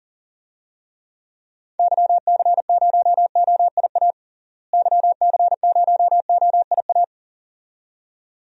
Morse code, heard as a keyed tone, about 40 wpm, 700 Hz. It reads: YC0OIA YC0OIA
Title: DXCC-100 Call Signs - 6 Characters - Repeated 2x - 40wpm